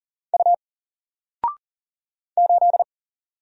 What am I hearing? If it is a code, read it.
Morse code, heard as a keyed tone, about 40 wpm, 700 Hz. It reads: U 8
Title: Single Letter-Number 40wpm